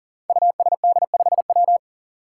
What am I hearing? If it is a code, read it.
Morse code, heard as a keyed tone, about 38 wpm, 700 Hz. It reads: USD5W
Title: Five Challenging Letter-Number Combinations 40wpm